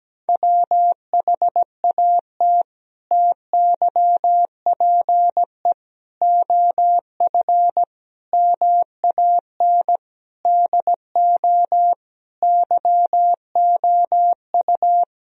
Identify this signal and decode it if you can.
Morse code, heard as a keyed tone, about 17 wpm, 700 Hz. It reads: WHAT TYPE OF MAN DO YOU
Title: Sentences from Top 1000 Words - Review Part 3 - 17wpm